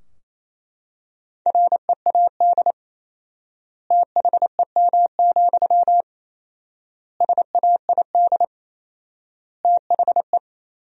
Morse code, heard as a keyed tone, about 28 wpm, 700 Hz. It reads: READ THEM, SAID THE
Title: 4x Word-Spacing - Sentences from Top 400 Words - 28wpm